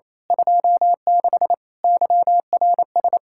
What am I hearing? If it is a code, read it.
Morse code, heard as a keyed tone, about 28 wpm, 700 Hz. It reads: 26 YRS
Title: Speed-Racing Common QSO Phrases - 28wpm